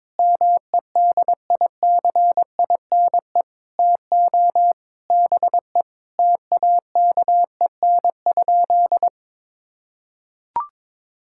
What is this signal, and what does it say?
Morse code, heard as a keyed tone, about 22 wpm, 700 Hz. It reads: MEDICINE TO BE TAKEN?
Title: Sentences from Top 1000 Words - Review Part 6 - 22wpm